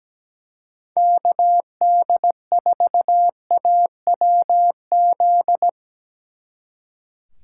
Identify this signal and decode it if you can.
Morse code, heard as a keyed tone, about 19 wpm, 700 Hz. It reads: KD4AWZ
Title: Speed Racing - USA 2x3 Call Signs 17wpm